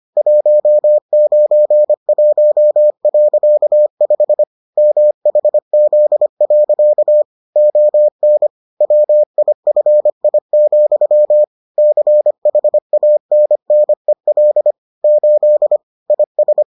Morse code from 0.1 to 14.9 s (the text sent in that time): 191.5 MHZ. ON WIFI, CHANNEL